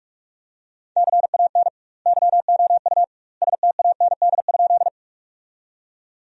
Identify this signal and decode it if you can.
Morse code, heard as a keyed tone, about 41 wpm, 700 Hz. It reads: CAN YOU STAND?
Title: 2x Word-Spacing - Sentences from Top 300 Words - 45wpm